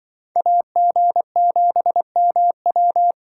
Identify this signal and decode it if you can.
Morse code, heard as a keyed tone, about 25 wpm, 700 Hz. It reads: AG7MW